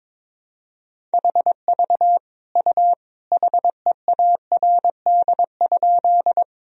Morse code, heard as a keyed tone, about 22 wpm, 700 Hz. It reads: HV U HEARD?
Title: Sentences from Top 400 Words and Abbreviations 22wpm